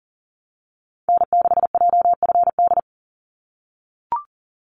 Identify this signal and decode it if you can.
Morse code, heard as a keyed tone, about 40 wpm, 700 Hz. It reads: N6JFD
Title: NA SOTA Activator Callsigns 40wpm